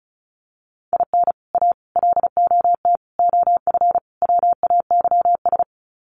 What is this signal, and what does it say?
Morse code, heard as a keyed tone, about 34 wpm, 700 Hz. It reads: IN A LOT OF WAYS
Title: Sets of 5 Words Encore 35wpm